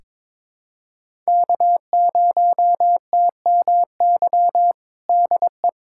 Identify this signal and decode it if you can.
Morse code, heard as a keyed tone, about 22 wpm, 700 Hz. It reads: K0TMY DE
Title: USA Callsign DE Callsign 22wpm